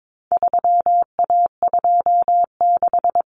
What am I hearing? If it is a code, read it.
Morse code, heard as a keyed tone, about 22 wpm, 700 Hz. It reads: 3A26
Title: Common Cut Numbers - 4-Digit 22wpm